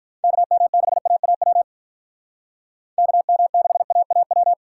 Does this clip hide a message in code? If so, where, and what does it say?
Yes: Morse code from 0.2 to 4.5 s, heard as a keyed tone, about 46 wpm, 700 Hz. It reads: KM6AAW KM6AAW